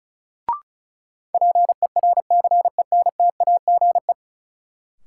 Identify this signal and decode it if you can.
Morse code, heard as a keyed tone, about 34 wpm, 700 Hz. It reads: PERCENTAGE
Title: Sentences from Top 1300 Words - New Word Drill Alt - 35wpm